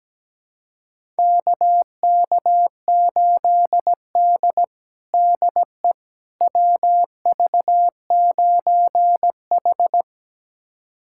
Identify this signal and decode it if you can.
Morse code, heard as a keyed tone, about 17 wpm, 700 Hz. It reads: KK8D DE WV9H
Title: USA Callsign DE Callsign - 4 Characters - 17wpm